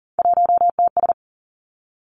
Morse code, heard as a keyed tone, about 42 wpm, 700 Hz. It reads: 1TS
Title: Three Challenging Letter-Number Combinations 40wpm